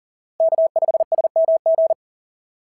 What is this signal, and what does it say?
Morse code, heard as a keyed tone, about 38 wpm, 680 Hz. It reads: K5SMG
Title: SOTA QSO Practice 40wpm